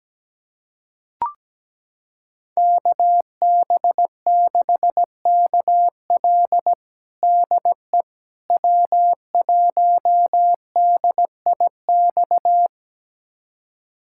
Morse code, heard as a keyed tone, about 17 wpm, 700 Hz. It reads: KB6KL DE W1DIX
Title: USA Callsign DE Callsign 17wpm